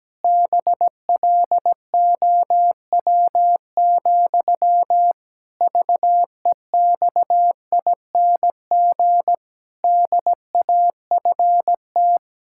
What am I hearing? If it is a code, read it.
Morse code, heard as a keyed tone, about 17 wpm, 700 Hz. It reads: BLOW, VEXING DAFT
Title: Pangram Challenge v2 - 17wpm